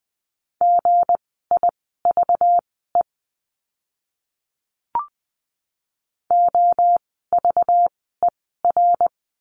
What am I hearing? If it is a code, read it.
Morse code, heard as a keyed tone, about 20 wpm, spaced as for 14 wpm, 700 Hz. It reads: GIVE OVER
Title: 230 Lesson 37f - Letter x - Words 20wpm